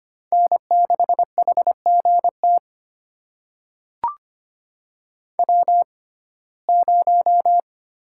Morse code, heard as a keyed tone, about 25 wpm, spaced as for 18 wpm, 700 Hz. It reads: N6HGT W0
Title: USA Call Signs - 5 Characters - 6x Character Spacing - 25wpm